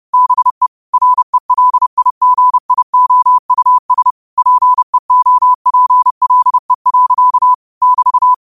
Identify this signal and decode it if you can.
Morse code, heard as a keyed tone, about 30 wpm, 1000 Hz. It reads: BE RELIGIOUS PEOPLE. <BT>
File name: qotdfiles / 30wpm